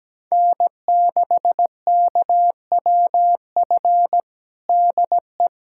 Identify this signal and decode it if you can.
Morse code, heard as a keyed tone, about 17 wpm, 700 Hz. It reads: N6KWF DE
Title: USA Callsign DE Callsign - 5 Characters - 17wpm